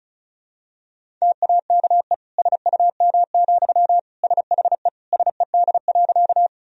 Morse code, heard as a keyed tone, about 34 wpm, 700 Hz. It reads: TAKE SUM, SHE SED.
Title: Sentences from Top 200 Words and Abbreviations 35wpm